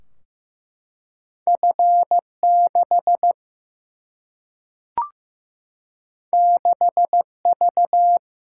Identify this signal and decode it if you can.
Morse code, heard as a keyed tone, about 15 wpm, 700 Hz. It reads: F 6 6 V
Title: Single-Letter Single-Number 15wpm